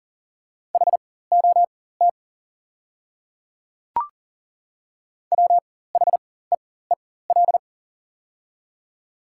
Morse code, heard as a keyed tone, about 40 wpm, spaced as for 19 wpm, 690 Hz. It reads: HOT WHEEL
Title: 4x Character Spacing - Top 500 Words - 40wpm